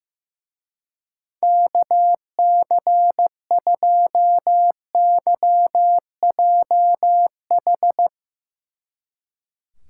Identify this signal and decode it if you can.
Morse code, heard as a keyed tone, about 17 wpm, 700 Hz. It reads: KC2YJH
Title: Speed Racing - USA 2x3 Call Signs 15wpm